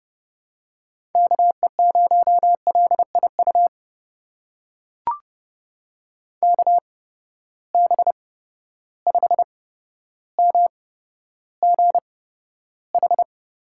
Morse code, heard as a keyed tone, about 30 wpm, spaced as for 13 wpm, 700 Hz. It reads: KE0LIU KB5MGH
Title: USA Call Signs - 6 Characters - 8x Character Spacing - 30wpm